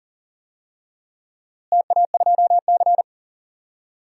Morse code, heard as a keyed tone, about 37 wpm, 700 Hz. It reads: TA2C
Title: DXCC-100 Call Signs - 4 Characters - Repeated 3x - 40wpm